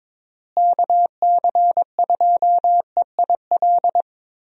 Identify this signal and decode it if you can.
Morse code, heard as a keyed tone, about 22 wpm, 700 Hz. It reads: KC2EIL
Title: USA Call Signs - 6 Characters - Repeated 2x - 22wpm